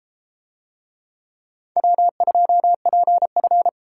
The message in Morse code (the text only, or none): W2PF